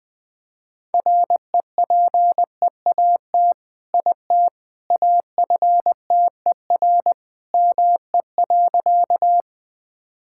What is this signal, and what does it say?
Morse code, heard as a keyed tone, about 20 wpm, 700 Hz. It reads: REPEAT IT AFTER ME.
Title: Sentences from Top 1600 Words 20wpm